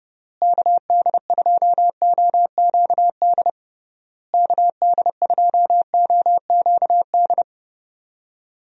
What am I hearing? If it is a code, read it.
Morse code, heard as a keyed tone, about 30 wpm, 700 Hz. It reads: KD2OQD KD2OQD
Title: USA Call Signs - 6 Characters - Repeated 3x - 30wpm